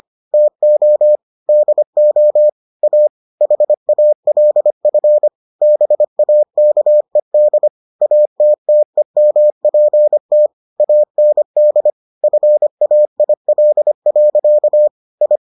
Morse code, heard as a keyed tone, about 25 wpm, 600 Hz. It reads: TO DO A HALF BAKED ATTEMPT AND FAIL. I